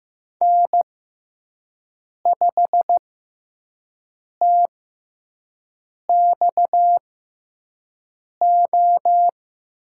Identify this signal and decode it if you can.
Morse code, heard as a keyed tone, about 15 wpm, spaced as for 5 wpm, 700 Hz. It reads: N5TXO
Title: USA Call Signs - 5 Characters - 6x Character Spacing - 15wpm